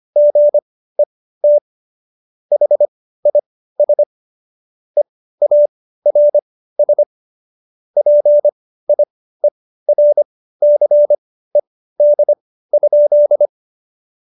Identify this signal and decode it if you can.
Morse code, heard as a keyed tone, about 25 wpm, spaced as for 15 wpm, 590 Hz. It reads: GET HIS EARS PIERCED?
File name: Jokes-25-char-15-eff